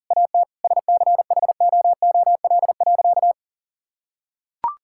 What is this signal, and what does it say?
Morse code, heard as a keyed tone, about 37 wpm, 700 Hz. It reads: AT SCHOOL.
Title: Sentences from Top 1000 Words - Review Part 6 - 40wpm